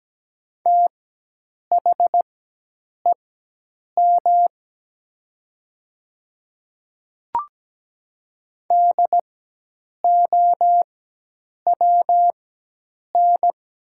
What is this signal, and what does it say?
Morse code, heard as a keyed tone, about 17 wpm, spaced as for 8 wpm, 700 Hz. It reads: THEM DOWN
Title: 4x Character Spacing - Top 100 Words - 17wpm